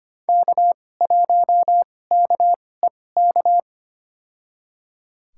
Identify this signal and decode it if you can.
Morse code, heard as a keyed tone, about 25 wpm, spaced as for 18 wpm, 700 Hz. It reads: K1KEK
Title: USA Call Signs - 5 Characters - 2x Character Spacing - 25wpm